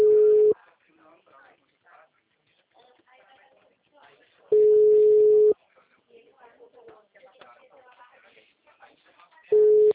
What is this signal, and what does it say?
Ringback tone (Europe, 425 Hz), with speech.
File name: out- -2011- - - .78165